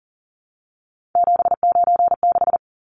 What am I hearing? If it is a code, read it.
Morse code, heard as a keyed tone, about 40 wpm, 700 Hz. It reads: 796
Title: Numbers - 3-Digit 40wpm